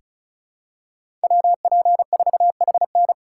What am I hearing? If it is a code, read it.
Morse code, heard as a keyed tone, about 36 wpm, 700 Hz. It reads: WP4HN
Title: USA Call Signs - 5 Characters - 4x Character Spacing - 35wpm